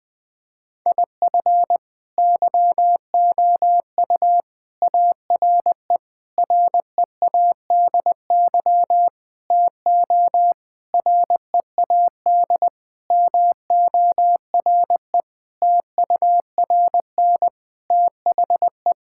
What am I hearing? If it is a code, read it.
Morse code, heard as a keyed tone, about 20 wpm, 700 Hz. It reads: IF YOU ARE READY TO READ MORE TURN THE
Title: Sentences with 6-Word Phrases - Spoken Beginning - 20wpm